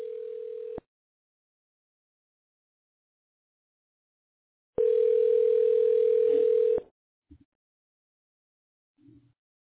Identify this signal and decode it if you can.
Ringback tone (North America), 440 + 480 Hz.